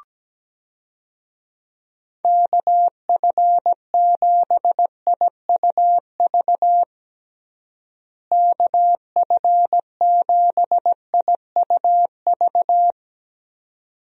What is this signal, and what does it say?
Morse code, heard as a keyed tone, about 17 wpm, 700 Hz. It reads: KF7IUV KF7IUV
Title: USA Call Signs - 6 Characters - Repeated 2x - 17wpm